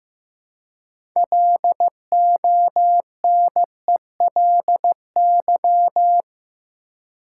Morse code, heard as a keyed tone, about 15 wpm, 700 Hz. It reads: LONELY
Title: Six Letter Words - 15wpm